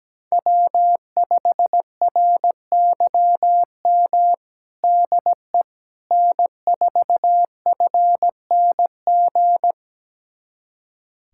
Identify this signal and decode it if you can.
Morse code, heard as a keyed tone, about 17 wpm, 700 Hz. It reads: W5RYM DE N4FNG
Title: USA Callsign DE Callsign - 5 Characters - 17wpm